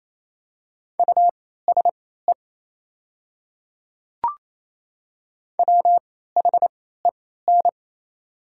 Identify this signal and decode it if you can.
Morse code, heard as a keyed tone, about 28 wpm, spaced as for 16 wpm, 700 Hz. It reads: USE WHEN
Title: 3x Character Spacing - Top 200 Words - 28wpm